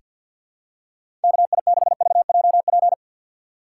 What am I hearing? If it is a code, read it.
Morse code, heard as a keyed tone, about 47 wpm, 700 Hz. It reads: KI6VJP
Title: USA Call Signs - 6 Characters - 3x Character Spacing - 50wpm